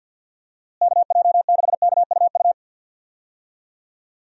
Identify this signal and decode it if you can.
Morse code, heard as a keyed tone, about 52 wpm, 700 Hz. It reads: KJ6KUU
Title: USA Call Signs - 6 Characters - Repeated 3x - 50wpm